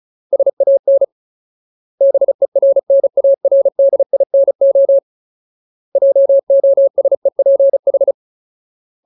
Morse code, heard as a keyed tone, about 34 wpm, 550 Hz. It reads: SAN BERNARDINO JOSEPH
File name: Ultimate Word List / 35WPM Ultimate